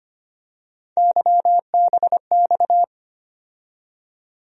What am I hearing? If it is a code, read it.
Morse code, heard as a keyed tone, about 25 wpm, 700 Hz. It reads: YBX
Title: USA Call Sign Postfixes - 3 Letters 25wpm